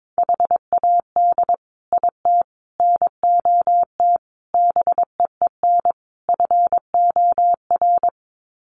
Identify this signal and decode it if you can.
Morse code, heard as a keyed tone, about 22 wpm, 700 Hz. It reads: HAD IT NOT BEEN FOR
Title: Sets of 5 Words Encore2 22wpm